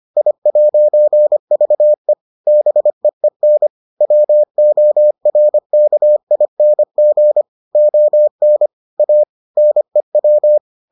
Morse code, heard as a keyed tone, about 25 wpm, 600 Hz. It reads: I'VE BEEN WORKING ON A NEW